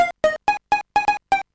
There is no code.